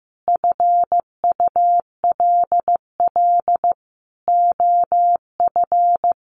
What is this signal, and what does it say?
Morse code, heard as a keyed tone, about 15 wpm, 700 Hz. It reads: FULL OF
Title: Sentences from Top 1300 Words 15wpm